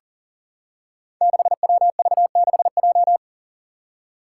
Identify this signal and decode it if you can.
Morse code, heard as a keyed tone, about 43 wpm, 700 Hz. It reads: 6WV6J
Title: Five Challenging Letter-Number Combinations 40wpm